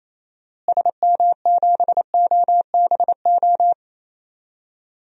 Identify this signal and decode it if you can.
Morse code, heard as a keyed tone, about 28 wpm, 700 Hz. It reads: SM7OBO
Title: DXCC-100 Call Signs - 6 Characters - Repeated 3x - 28wpm